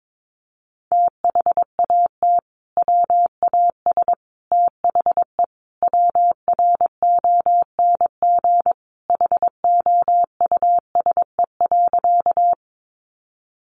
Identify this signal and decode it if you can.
Morse code, heard as a keyed tone, about 22 wpm, 700 Hz. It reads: THAT WAS THE WRONG HOUSE.
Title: Sentences from Top 900 Words 22wpm